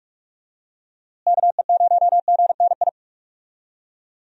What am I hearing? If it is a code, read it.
Morse code, heard as a keyed tone, about 46 wpm, 700 Hz. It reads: KE0GNI
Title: USA Call Signs - 6 Characters - Repeated 2x - 45wpm